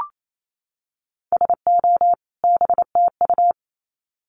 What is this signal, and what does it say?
Morse code, heard as a keyed tone, about 28 wpm, 700 Hz. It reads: SO BTU
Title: Common QSO Phrases 28wpm